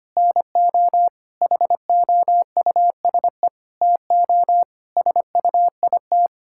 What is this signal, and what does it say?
Morse code, heard as a keyed tone, about 25 wpm, 700 Hz. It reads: NO HOUSE TO SUIT
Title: Sentences from Top 1600 Words 25wpm